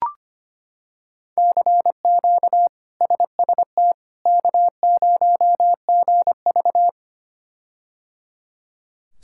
Morse code, heard as a keyed tone, about 25 wpm, 700 Hz. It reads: CQ SST K0GV
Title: K1USN SST QSO Practice - 25wpm